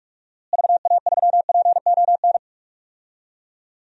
Morse code, heard as a keyed tone, about 42 wpm, 700 Hz. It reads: VA3PON
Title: DXCC-100 Call Signs - 6 Characters - Repeated 3x - 45wpm